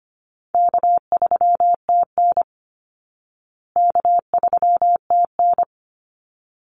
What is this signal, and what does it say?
Morse code, heard as a keyed tone, about 25 wpm, 700 Hz. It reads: K3TN K3TN
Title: Easy Call Signs Repeated 2x - 25wpm